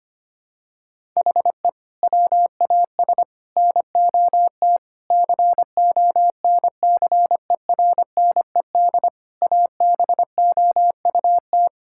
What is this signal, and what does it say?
Morse code, heard as a keyed tone, about 25 wpm, 700 Hz. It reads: HE WAS NOT CONCERNED ABOUT
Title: Sentences from Top 1000 Words - Review Part 6 - 25wpm